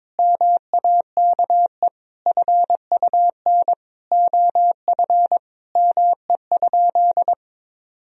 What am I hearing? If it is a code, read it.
Morse code, heard as a keyed tone, about 21 wpm, 700 Hz. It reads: MAKE FUN OF ME?
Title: Sentences from Top 1000 Words - Review Part 4 - 22wpm